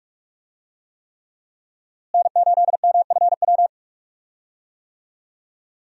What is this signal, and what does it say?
Morse code, heard as a keyed tone, about 46 wpm, 700 Hz. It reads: N8MFW
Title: USA Call Signs - 5 Characters - Repeated 3x - 45wpm